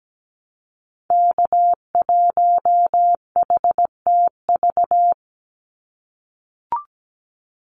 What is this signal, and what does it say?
Morse code, heard as a keyed tone, about 17 wpm, 700 Hz. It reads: K1HTV
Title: Call Signs - 17wpm